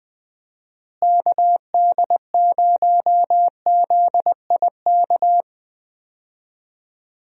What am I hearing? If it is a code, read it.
Morse code, heard as a keyed tone, about 20 wpm, 700 Hz. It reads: KD0ZIK
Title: USA Call Signs - 6 Characters - Repeated 3x - 20wpm